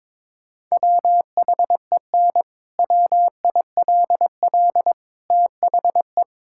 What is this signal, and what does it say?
Morse code, heard as a keyed tone, about 21 wpm, 700 Hz. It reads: WHEN WILL THE
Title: Sentences from Top 1000 Words - Review Part 6 - 22wpm